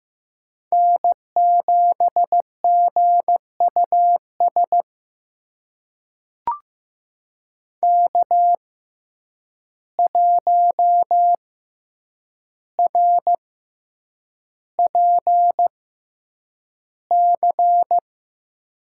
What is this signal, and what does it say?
Morse code, heard as a keyed tone, about 15 wpm, spaced as for 8 wpm, 700 Hz. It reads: N7GUS K1RPC
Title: USA Call Signs - 5 Characters - 6x Character Spacing - 15wpm